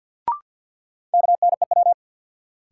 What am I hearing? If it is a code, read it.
Morse code, heard as a keyed tone, about 50 wpm, 700 Hz. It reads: KNEW
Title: Sentences from Top 400 Words - New Word Drill - 50wpm